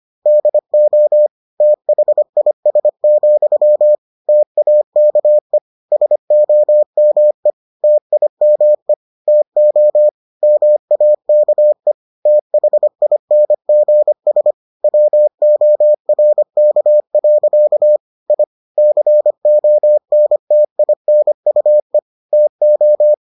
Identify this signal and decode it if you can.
Morse code, heard as a keyed tone, about 25 wpm, 600 Hz. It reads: DO THIS, TAKE SOME TIME TO MAKE THINGS WORK. I CONTINUE TO